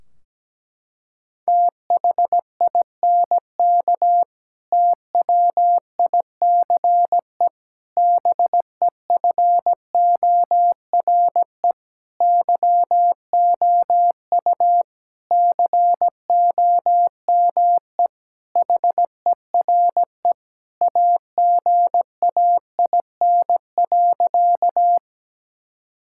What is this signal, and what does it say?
Morse code, heard as a keyed tone, about 17 wpm, 700 Hz. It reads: THINK TWICE BEFORE YOU COME HERE AGAIN.
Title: Sentences from Top 1600 Words 17wpm